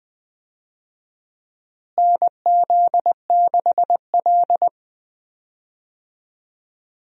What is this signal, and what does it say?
Morse code, heard as a keyed tone, about 21 wpm, 700 Hz. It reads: NZ6L
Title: USA Call Signs - 4 Characters - Repeated 2x - 20wpm